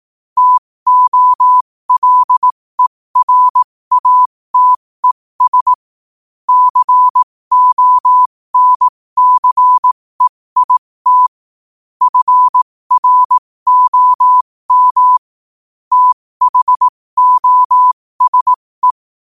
Morse code, heard as a keyed tone, about 18 wpm, spaced as for 15 wpm, 1000 Hz. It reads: TOLERATES CONCEIT FROM THOSE
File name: qotdfiles / 15wpm